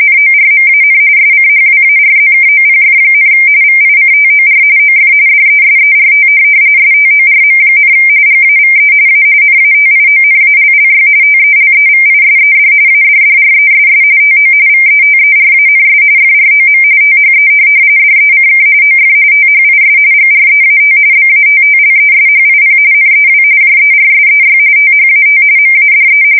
RTTY, about 45.45 baud. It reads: NORMAN SCHWARZKOPF, VICTOR OF DESERT STORM, GAVE A NATIONALLY TELEVISED ADDRESS TO VETERANS FROM THE DECK OF THIS GREAT BATTLESHIP NEW JERSEY THAT MADE
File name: fsk170 / USS-New-Jersey